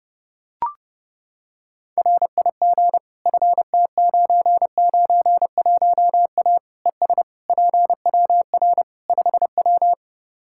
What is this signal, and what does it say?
Morse code, heard as a keyed tone, about 30 wpm, 700 Hz. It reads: RIG FT991A ES PWR 5W
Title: Practice QSO - Exchange 2 - Info - 30wpm